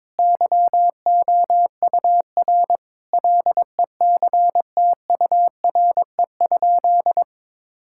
Morse code, heard as a keyed tone, about 22 wpm, 700 Hz. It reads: YOUR LECTURE?
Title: Sentences from Top 1600 Words 22wpm